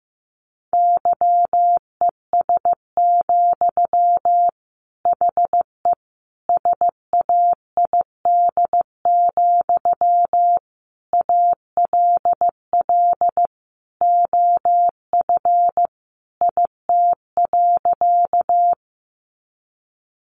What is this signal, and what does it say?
Morse code, heard as a keyed tone, about 15 wpm, 700 Hz. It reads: YES, HE SAID, ALL OF IT.
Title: Sentences from Top 500 Words 15wpm